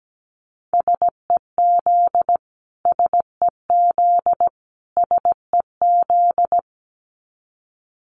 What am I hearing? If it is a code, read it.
Morse code, heard as a keyed tone, about 17 wpm, 700 Hz. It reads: SEZ SEZ SEZ
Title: Abbreviations Repeated 3x - 17wpm